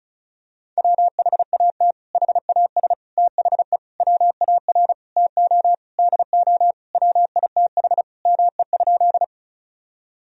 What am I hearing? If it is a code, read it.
Morse code, heard as a keyed tone, about 36 wpm, 700 Hz. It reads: WHAT HAS THE WAR TO DO WITH ME?
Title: Sentences from Top 1000 Words - Review Part 2 - 35wpm